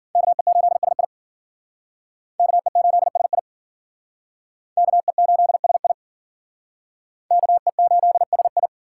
Morse code, heard as a keyed tone, about 51 wpm, 700 Hz. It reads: KE8SI KE8SI KE8SI KE8SI